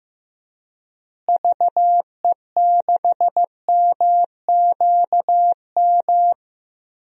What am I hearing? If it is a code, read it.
Morse code, heard as a keyed tone, about 16 wpm, 700 Hz. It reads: VE6MQM
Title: DXCC-100 Call Signs - 6 Characters - Repeated 2x - 15wpm